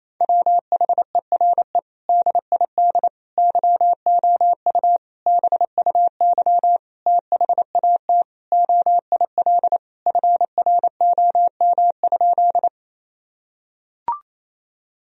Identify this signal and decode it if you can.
Morse code, heard as a keyed tone, about 28 wpm, 700 Hz. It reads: WHERE DID YOU BUY THAT OIL FROM?
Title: Sentences from Top 1000 Words - Review Part 4 - 28wpm